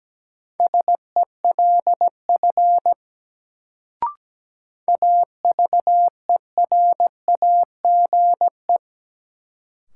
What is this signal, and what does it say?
Morse code, heard as a keyed tone, about 17 wpm, 700 Hz. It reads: SELF AVERAGE
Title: Sentences from Top 1400 Words - New Word Drill Alt - 17wpm